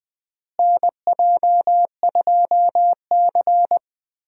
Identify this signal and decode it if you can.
Morse code, heard as a keyed tone, about 20 wpm, 690 Hz. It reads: NJ2C
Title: USA Call Signs - 4 Characters - Repeated 3x - 20wpm